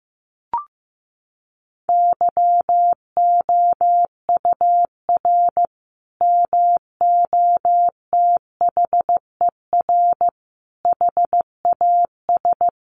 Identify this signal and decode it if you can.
Morse code, heard as a keyed tone, about 15 wpm, 700 Hz. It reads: YOUR MOTHER HAS
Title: Sentences from Top 1200 Words 15wpm